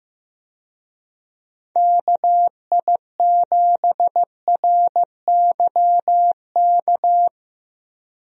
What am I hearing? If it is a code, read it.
Morse code, heard as a keyed tone, about 15 wpm, 700 Hz. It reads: KI7RYK
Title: USA Call Signs - 6 Characters - Repeated 2x - 15wpm